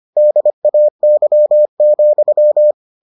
Morse code, heard as a keyed tone, about 25 wpm, 600 Hz. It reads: DAY,